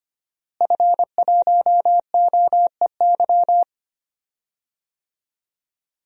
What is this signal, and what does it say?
Morse code, heard as a keyed tone, about 25 wpm, 700 Hz. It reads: F1OEY
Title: DXCC-100 Call Signs - 5 Characters - Repeated 3x - 25wpm